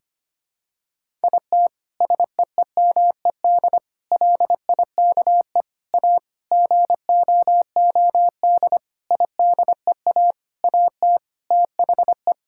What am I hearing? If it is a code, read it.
Morse code, heard as a keyed tone, about 25 wpm, 700 Hz. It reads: IT SEEMED LIKE A GOOD IDEA AT THE
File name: practice / 25wpm / Sentences with 9-Word Phrases - 25wpm